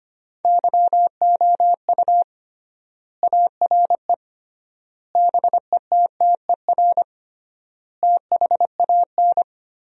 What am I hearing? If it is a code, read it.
Morse code, heard as a keyed tone, about 25 wpm, 700 Hz. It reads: YOU ARE BETTER THAN
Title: 3x Word-Spacing - Sentences from Top 400 Words - 25wpm